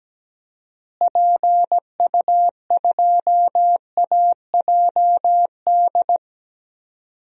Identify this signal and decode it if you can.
Morse code, heard as a keyed tone, about 17 wpm, 700 Hz. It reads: PU2AJD
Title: DXCC-100 Call Signs - 6 Characters - Repeated 3x - 17wpm